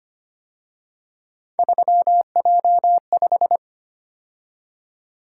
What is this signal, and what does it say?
Morse code, heard as a keyed tone, about 31 wpm, 700 Hz. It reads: 3J5
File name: Speed-Racing - Three Letter-Number Combinations - 25wpm